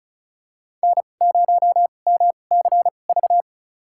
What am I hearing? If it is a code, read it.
Morse code, heard as a keyed tone, about 36 wpm, spaced as for 25 wpm, 700 Hz. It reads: N0MCV
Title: USA Call Signs - 5 Characters - 2x Character Spacing - 35wpm